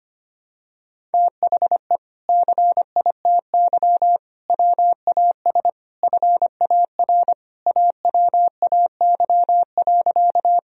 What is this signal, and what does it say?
Morse code, heard as a keyed tone, about 25 wpm, 700 Hz. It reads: THE CITY WAS FAR AWAY.
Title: Sentences from Top 1000 Words - Review Part 1 - 25wpm